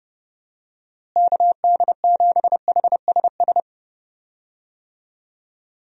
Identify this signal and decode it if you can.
Morse code, heard as a keyed tone, about 32 wpm, 700 Hz. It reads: KD7HSS
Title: USA Call Signs - 6 Characters - Repeated 2x - 30wpm